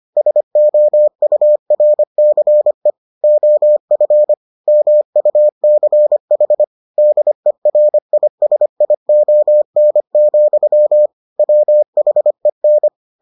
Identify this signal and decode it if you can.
Morse code, heard as a keyed tone, about 25 wpm, 600 Hz. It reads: SOURCE OF MUCH DERISION, WHEN